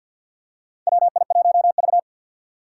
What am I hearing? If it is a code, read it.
Morse code, heard as a keyed tone, about 53 wpm, 700 Hz. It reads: WI1V
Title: USA Call Signs - 6x Character Spacing - 50wpm